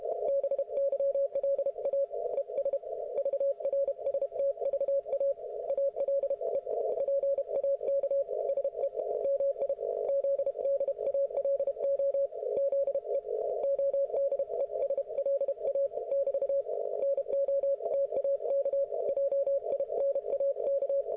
Morse code, heard as a keyed tone, about 32 wpm, 560 Hz. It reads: BYLA ES VRSTVA ALE PAK SE MI ZDALO ZE ODESLA <BT> NO TAK JINAK